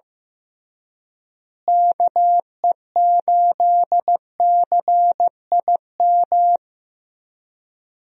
Morse code, heard as a keyed tone, about 15 wpm, 700 Hz. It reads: KE8CIM
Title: USA Call Signs - 6 Characters - Repeated 3x - 15wpm